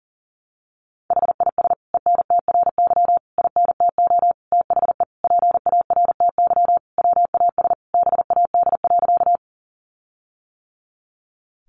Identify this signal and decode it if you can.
Morse code, heard as a keyed tone, about 40 wpm, 700 Hz. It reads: HIS ENTRY INTO THE PARTY WAS BAD.
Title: Sentences from Top 1500 Words 40wpm